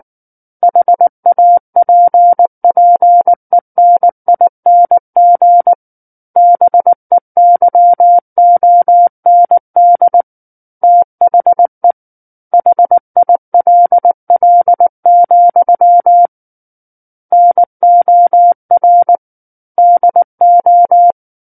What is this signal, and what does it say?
Morse code, heard as a keyed tone, about 19 wpm, 700 Hz. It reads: HAPPENING BEYOND THE HILL, NOR DO
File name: Book One / 10-Chapter 10 - 19 WPM